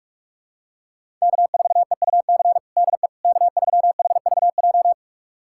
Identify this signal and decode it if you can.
Morse code, heard as a keyed tone, about 42 wpm, 700 Hz. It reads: K4EUC DE K3HVJ
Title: USA Callsign DE Callsign - 5 Characters - 45wpm